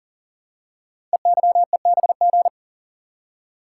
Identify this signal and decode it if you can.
Morse code, heard as a keyed tone, about 44 wpm, 700 Hz. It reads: EYEBG
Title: Code Groups - 5 Letters - 40wpm